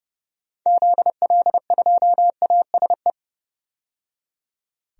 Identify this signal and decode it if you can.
Morse code, heard as a keyed tone, about 31 wpm, 700 Hz. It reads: ZL2ASE
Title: DXCC-100 Call Signs - 6 Characters - 30wpm